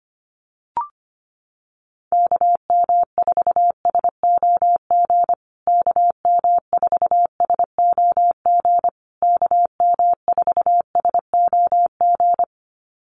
Morse code, heard as a keyed tone, about 25 wpm, 700 Hz. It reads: KM4SOG KM4SOG KM4SOG
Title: USA Call Signs Repeated 3x 25wpm